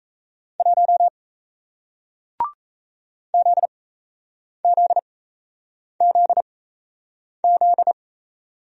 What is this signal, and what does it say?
Morse code, heard as a keyed tone, about 38 wpm, spaced as for 8 wpm, 700 Hz. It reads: 1 ZZZZ